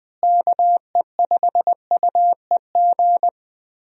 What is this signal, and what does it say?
Morse code, heard as a keyed tone, about 20 wpm, 700 Hz. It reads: KE5UEG
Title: USA Call Signs - 6 Characters - Repeated 2x - 20wpm